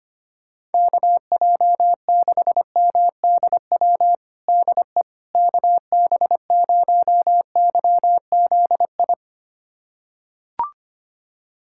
Morse code, heard as a keyed tone, about 25 wpm, 700 Hz. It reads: KJ6MDW DE KB0YZI
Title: USA Callsign DE Callsign 25wpm